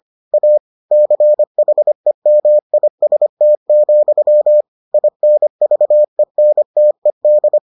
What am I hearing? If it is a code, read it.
Morse code, heard as a keyed tone, about 25 wpm, 600 Hz. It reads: A CHEMIST, INVENTED